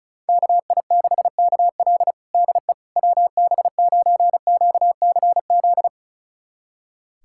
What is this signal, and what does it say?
Morse code, heard as a keyed tone, about 34 wpm, 700 Hz. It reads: KI6KL DE WB9QCZ
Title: USA Callsign DE Callsign 35wpm